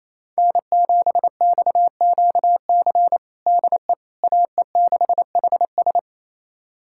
Morse code, heard as a keyed tone, about 28 wpm, 700 Hz. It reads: N7XQC DE AE6HS
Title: USA Callsign DE Callsign - 5 Characters - 28wpm